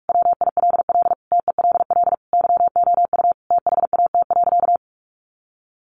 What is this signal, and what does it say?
Morse code, heard as a keyed tone, about 44 wpm, 700 Hz. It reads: WILL TELL YOU THAT.
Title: Sentences from Top 300 Words 45wpm